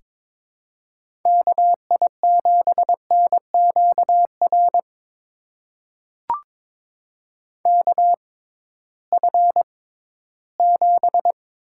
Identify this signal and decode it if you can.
Morse code, heard as a keyed tone, about 22 wpm, spaced as for 14 wpm, 700 Hz. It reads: KI7NQR KF7
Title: USA Call Signs - 6 Characters - 6x Character Spacing - 22wpm